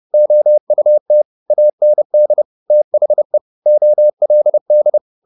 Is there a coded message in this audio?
Yes: Morse code from 0.1 to 5.0 s, heard as a keyed tone, about 30 wpm, 600 Hz. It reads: OUT AND THE OLD